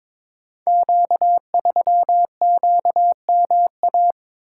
Morse code, heard as a keyed tone, about 22 wpm, 700 Hz. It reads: Q3QMA
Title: Code Groups - 5 Letter:Numbers - 22wpm